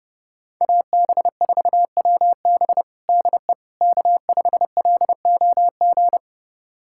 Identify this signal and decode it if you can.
Morse code, heard as a keyed tone, about 32 wpm, 700 Hz. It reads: AB4WB DE K5LOG
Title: USA Callsign DE Callsign - 5 Characters - 30wpm